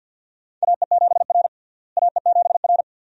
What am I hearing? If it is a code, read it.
Morse code, heard as a keyed tone, about 52 wpm, 700 Hz. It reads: AE7R AE7R
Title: USA Call Signs - 4 Characters - Repeated 2x - 50wpm